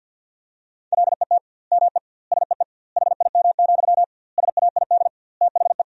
Morse code, heard as a keyed tone, about 50 wpm, 700 Hz. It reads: LET ME SEE HIM, SAID THE
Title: 2x Word-Spacing - Sentences from Top 400 Words - 50wpm